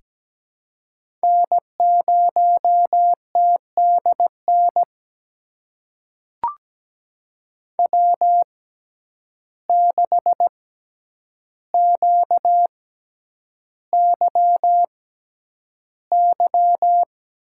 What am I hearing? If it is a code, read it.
Morse code, heard as a keyed tone, about 17 wpm, spaced as for 9 wpm, 700 Hz. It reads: N0TDN W6QYY
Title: USA Call Signs - 5 Characters - 6x Character Spacing - 17wpm